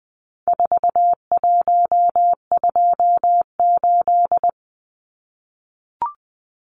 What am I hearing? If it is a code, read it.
Morse code, heard as a keyed tone, about 20 wpm, 700 Hz. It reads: 4128
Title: Numbers - 4-Digit 20wpm